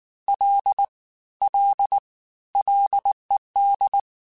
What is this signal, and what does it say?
Morse code, heard as a keyed tone, about 19 wpm, 800 Hz. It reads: L L LED
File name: Ls / Single L m3s / Single L Prefixe 3x2 Twice @18